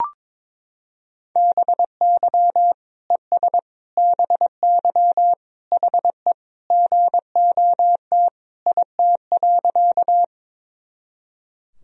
Morse code, heard as a keyed tone, about 22 wpm, 700 Hz. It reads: BY ES BY HE GOT IT.
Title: Sentences from Top 300 Words and Abbreviations 22wpm